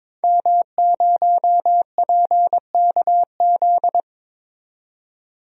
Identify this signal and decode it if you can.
Morse code, heard as a keyed tone, about 22 wpm, 700 Hz. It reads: M0PKZ
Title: DXCC-100 Call Signs - 5 Characters - 22wpm